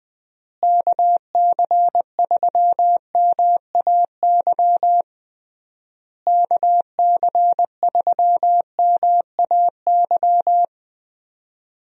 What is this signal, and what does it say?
Morse code, heard as a keyed tone, about 21 wpm, 700 Hz. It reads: KC3MAY KC3MAY
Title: USA Call Signs - 6 Characters - Repeated 3x - 20wpm